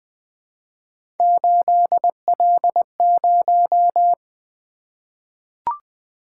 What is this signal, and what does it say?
Morse code, heard as a keyed tone, about 20 wpm, 700 Hz. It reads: 8L0
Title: Three Letter-Number Combinations - 20wpm